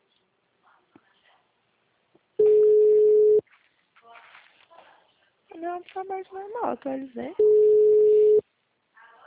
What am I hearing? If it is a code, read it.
Ringback tone (Europe, 425 Hz), with speech.